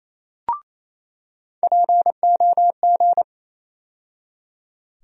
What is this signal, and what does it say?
Morse code, heard as a keyed tone, about 28 wpm, 700 Hz. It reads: POG
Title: USA Call Sign Postfixes - 3 Letters 28wpm